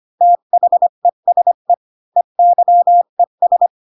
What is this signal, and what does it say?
Morse code, heard as a keyed tone, about 25 wpm, 700 Hz. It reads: THESE EYES
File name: Pinocchio / Pinocchio ch3 20 25 ik1hge